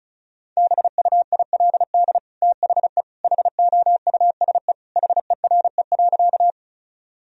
Morse code, heard as a keyed tone, about 36 wpm, 700 Hz. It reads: BUILD THE HOUSE HERE.
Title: Sentences from Top 1000 Words - Review Part 1 - 35wpm